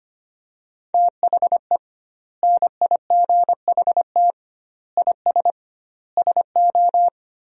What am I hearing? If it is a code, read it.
Morse code, heard as a keyed tone, about 25 wpm, 700 Hz. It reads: THE NIGHT IS SO
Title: 2x Word-Spacing - Sentences from Top 500 Words - 25wpm